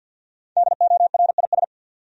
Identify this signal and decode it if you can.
Morse code, heard as a keyed tone, about 51 wpm, 700 Hz. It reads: DORIS
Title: USA Common Names - Max 5 Letters - 50wpm